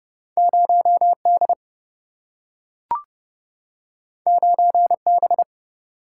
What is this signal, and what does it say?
Morse code, heard as a keyed tone, about 29 wpm, 700 Hz. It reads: 0D 9B
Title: Single-Letter Single-Number 30wpm